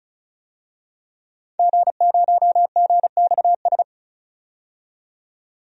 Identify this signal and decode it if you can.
Morse code, heard as a keyed tone, about 36 wpm, 700 Hz. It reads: G0GXS
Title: DXCC-100 Call Signs - 5 Characters - 35wpm